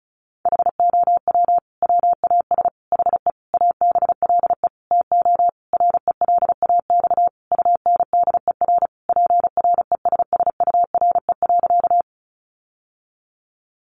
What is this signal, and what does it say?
Morse code, heard as a keyed tone, about 34 wpm, 700 Hz. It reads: HOW WAS HE ABLE TO RELAX UNDER PRESSURE.
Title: Sentences from Top 1400 Words 35wpm